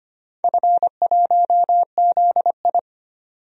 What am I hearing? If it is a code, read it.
Morse code, heard as a keyed tone, about 25 wpm, 700 Hz. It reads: F1ZI
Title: DXCC-100 Call Signs - 4 Characters - 25wpm